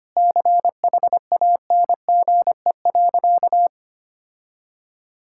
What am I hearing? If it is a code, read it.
Morse code, heard as a keyed tone, about 25 wpm, 700 Hz. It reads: CHANGE.
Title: Sentences from Top 1000 Words - Review Part 5 - 25wpm